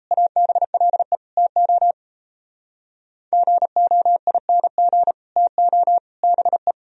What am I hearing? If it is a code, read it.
Morse code, heard as a keyed tone, about 35 wpm, 700 Hz. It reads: ABLE TO GOING TO BE